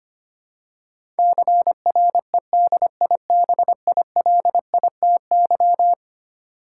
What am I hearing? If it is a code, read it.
Morse code, heard as a keyed tone, about 25 wpm, 700 Hz. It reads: CREDIBILITY
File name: practice / 25wpm / Eleven Letter Words - 25wpm